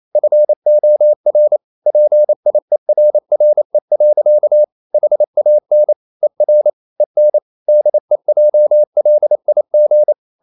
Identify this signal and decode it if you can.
Morse code, heard as a keyed tone, about 28 wpm, 600 Hz. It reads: FOR PIERRE. HAN ER EN DEJLIG
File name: Ferie QRQ 28wpm